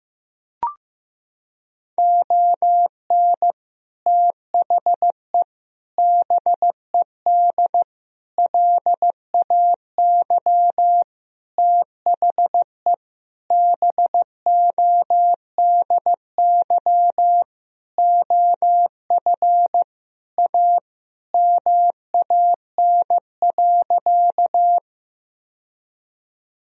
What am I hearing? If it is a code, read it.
Morse code, heard as a keyed tone, about 15 wpm, 700 Hz. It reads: ON THE BED LAY THE BODY OF A MAN.
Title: Sentences from Top 1000 Words - Review Part 2 - 15wpm